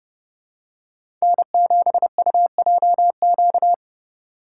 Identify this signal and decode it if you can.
Morse code, heard as a keyed tone, about 30 wpm, 700 Hz. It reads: N7UJQ
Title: USA Call Signs - 5 Characters - 3x Character Spacing - 30wpm